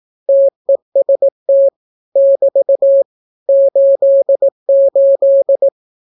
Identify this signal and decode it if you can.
Morse code, heard as a keyed tone, about 18 wpm, 550 Hz. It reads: TEST <BT> 88
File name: naukapodstaw / skroty / skroty